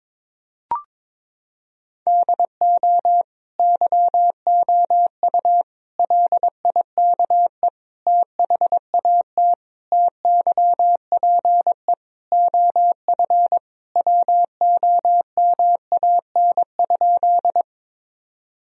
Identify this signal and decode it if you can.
Morse code, heard as a keyed tone, about 22 wpm, 700 Hz. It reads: DO YOU LIKE THAT TYPE OF WOMAN?
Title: Sentences from Top 1000 Words - Review Part 3 - 22wpm